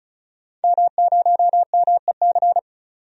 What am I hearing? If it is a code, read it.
Morse code, heard as a keyed tone, about 36 wpm, 700 Hz. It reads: M0MEC
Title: DXCC-100 Call Signs - 5 Characters - 35wpm